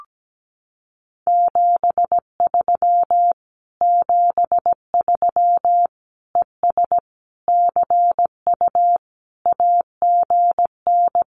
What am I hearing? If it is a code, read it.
Morse code, heard as a keyed tone, about 17 wpm, 700 Hz. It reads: 73 73 ES CU AGN
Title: Common QSO Phrases Encore 17wpm